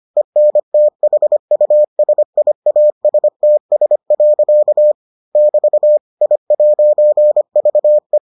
Morse code, heard as a keyed tone, about 25 wpm, 600 Hz. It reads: ENTHUSIASTS. <BT> I'VE